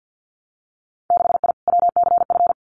Morse code, heard as a keyed tone, about 50 wpm, 700 Hz. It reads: 6I WCF
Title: Winter Field Day Exchanges Encore 50wpm